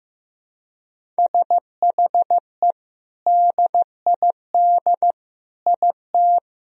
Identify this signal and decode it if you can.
Morse code, heard as a keyed tone, about 15 wpm, 700 Hz. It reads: SHE DID IT
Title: Sentences from Top 400 Words and Abbreviations 15wpm